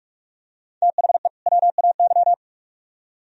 Morse code, heard as a keyed tone, about 44 wpm, 700 Hz. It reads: THE WAY